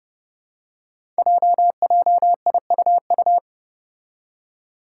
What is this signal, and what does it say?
Morse code, heard as a keyed tone, about 30 wpm, 700 Hz. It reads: JJIUU
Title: Five Challenging Letter-Number Combinations 30wpm